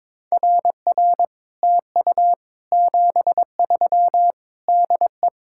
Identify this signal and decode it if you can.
Morse code, heard as a keyed tone, about 22 wpm, 700 Hz. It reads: RR TU 73 DE
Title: Speed-Racing Common QSO Phrases - 22wpm